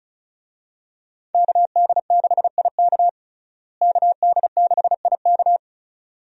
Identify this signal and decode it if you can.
Morse code, heard as a keyed tone, about 36 wpm, 700 Hz. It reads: KD6IK KD6IK
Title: USA Call Signs - 5 Characters - Repeated 2x - 35wpm